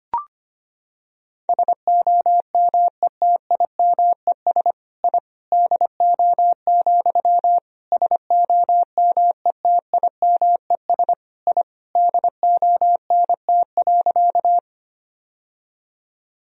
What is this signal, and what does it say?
Morse code, heard as a keyed tone, about 25 wpm, 700 Hz. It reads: SOMETIMES I DO, SOMETIMES I DONT.
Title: Sentences from Top 1000 Words - Review Part 3 - 25wpm